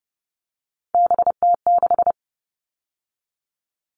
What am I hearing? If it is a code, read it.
Morse code, heard as a keyed tone, about 30 wpm, 700 Hz. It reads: BT6
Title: All Cut Numbers - 3-Digit 30wpm